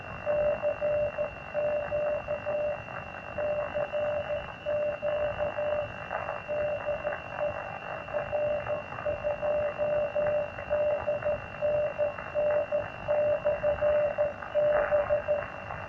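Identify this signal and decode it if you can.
Morse code, heard as a keyed tone, about 14 wpm, 600 Hz, with noise. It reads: CQ CQ DE R2DNN/B
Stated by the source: 120 mW QRP beacon received near Kyiv on an FT857 with a makeshift 10 m wire antenna, household appliance interference, recorded by phone.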